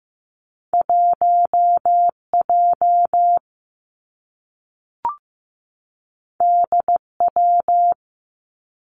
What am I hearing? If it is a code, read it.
Morse code, heard as a keyed tone, about 15 wpm, 700 Hz. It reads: 1 J D W
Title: Two Challenging Letter-Number Combinations 15wpm